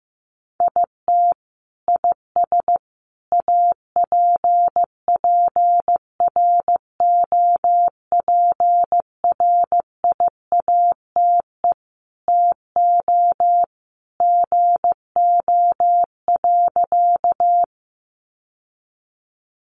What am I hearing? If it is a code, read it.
Morse code, heard as a keyed tone, about 15 wpm, 700 Hz. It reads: IT IS APPROPRIATE TO GO.
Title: Sentences from Top 900 Words 15wpm